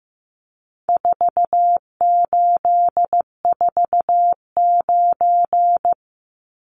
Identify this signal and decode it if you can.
Morse code, heard as a keyed tone, about 15 wpm, 700 Hz. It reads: 4849
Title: Numbers - 4-Digit 15wpm